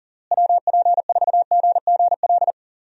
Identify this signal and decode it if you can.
Morse code, heard as a keyed tone, about 40 wpm, 700 Hz. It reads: WP4GGL
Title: USA Call Signs - 6 Characters - Repeated 3x - 40wpm